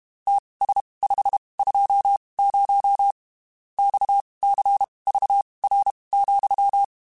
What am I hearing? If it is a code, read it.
Morse code, heard as a keyed tone, about 32 wpm, spaced as for 23 wpm, 800 Hz. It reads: TS520 XCVR,
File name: (spaced) / QSOs Spaced-mp3s / WD0FBT 1 @30 sp